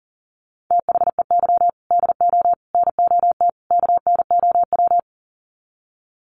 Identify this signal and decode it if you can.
Morse code, heard as a keyed tone, about 40 wpm, 700 Hz. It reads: THEY DO NOT KNOW
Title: Sets of 4 Words 40wpm